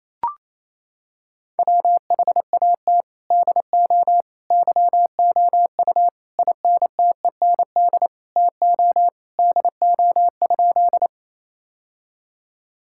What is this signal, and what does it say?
Morse code, heard as a keyed tone, about 28 wpm, 700 Hz. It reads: WHAT DO YOU INTEND TO DO?
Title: Sentences from Top 1600 Words 28wpm